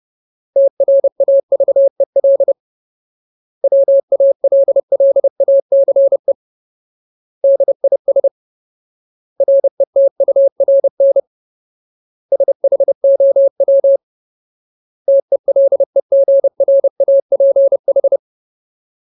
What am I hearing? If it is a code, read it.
Morse code, heard as a keyed tone, about 30 wpm, 550 Hz. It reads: TRAVEL WALLACE DIS RETURN SHOW TELEGRAPH
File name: Ultimate Word List / 30WPM Ultimate